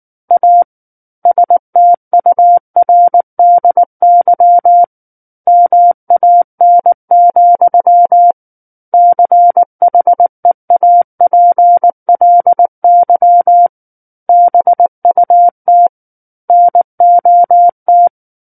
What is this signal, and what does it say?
Morse code, heard as a keyed tone, about 19 wpm, 700 Hz. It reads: A STURDY MAN, CHEAPLY BUT NOT
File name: Book One / 10-Chapter 10 - 19 WPM